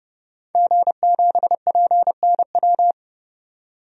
Morse code, heard as a keyed tone, about 30 wpm, 700 Hz. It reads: G7PNW
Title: DXCC-100 Call Signs - 5 Characters - 30wpm